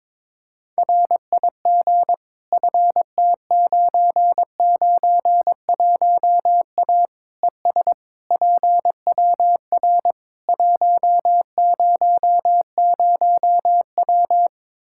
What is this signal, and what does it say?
Morse code, heard as a keyed tone, about 22 wpm, 700 Hz. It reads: RIG FT991A ES PWR 100W
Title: Practice QSO - Exchange 2 - Info - 22wpm